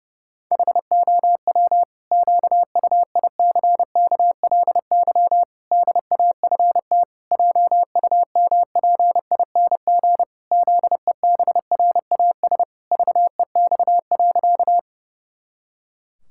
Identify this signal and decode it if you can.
Morse code, heard as a keyed tone, about 32 wpm, 700 Hz. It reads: HOW QUICKLY DAFT JUMPING ZEBRAS VEX.
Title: Pangram Challenge v2 - 30wpm